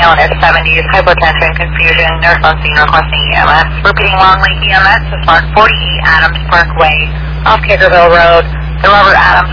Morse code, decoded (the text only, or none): none